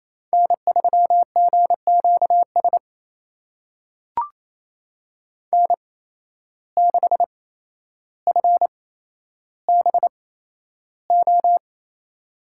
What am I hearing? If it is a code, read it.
Morse code, heard as a keyed tone, about 28 wpm, spaced as for 12 wpm, 700 Hz. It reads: N3GQS N6FBO
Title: USA Call Signs - 5 Characters - 8x Character Spacing - 28wpm